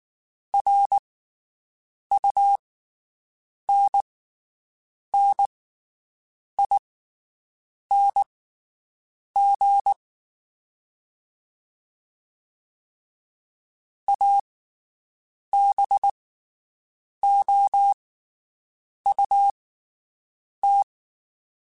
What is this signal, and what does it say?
Morse code, heard as a keyed tone, about 19 wpm, spaced as for 7 wpm, 800 Hz. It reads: RUNNING ABOUT
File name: QSOs Spaced